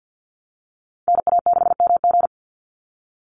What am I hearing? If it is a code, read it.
Morse code, heard as a keyed tone, about 50 wpm, 700 Hz. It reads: NA6MG
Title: NA SOTA Activator Callsigns 50wpm